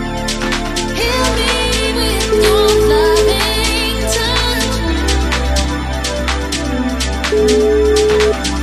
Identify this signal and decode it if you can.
Ringback tone (Europe, 425 Hz), with noise.